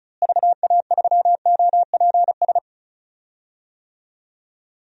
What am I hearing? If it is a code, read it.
Morse code, heard as a keyed tone, about 36 wpm, 700 Hz. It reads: VA3OPS
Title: DXCC-100 Call Signs - 6 Characters - Repeated 2x - 35wpm